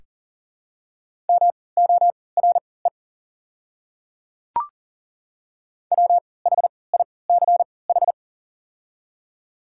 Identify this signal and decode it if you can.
Morse code, heard as a keyed tone, about 40 wpm, spaced as for 23 wpm, 690 Hz. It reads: MORE WHICH
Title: 3x Character Spacing - Top 100 Words - 40wpm